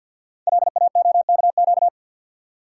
Morse code, heard as a keyed tone, about 46 wpm, 700 Hz. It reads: LAOKQ
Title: Code Groups - 5 Letters - 50wpm